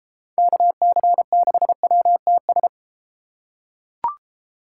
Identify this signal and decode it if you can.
Morse code, heard as a keyed tone, about 32 wpm, 700 Hz. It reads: KC6WTS